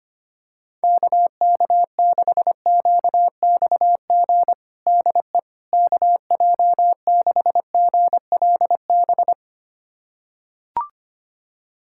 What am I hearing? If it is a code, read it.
Morse code, heard as a keyed tone, about 25 wpm, 700 Hz. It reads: KK6QXG DE KJ6GLB
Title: USA Callsign DE Callsign - 6 Characters - 25wpm